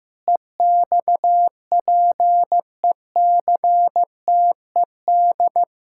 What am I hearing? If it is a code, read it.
Morse code, heard as a keyed tone, about 15 wpm, 700 Hz. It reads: EXPECTED
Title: Pangram Challenge v2 - 15wpm